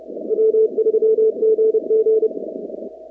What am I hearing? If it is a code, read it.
Morse code, heard as a keyed tone, about 28 wpm, 440 Hz. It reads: W3GG
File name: contests / 2019-cw-wpx-kt5j-k5pi- -2330 0-44